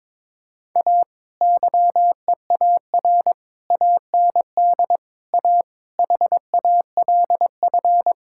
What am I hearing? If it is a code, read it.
Morse code, heard as a keyed tone, about 21 wpm, 700 Hz. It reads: A YEAR AND A HALF
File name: practice / 22wpm / Sets of 6 Words - 22wpm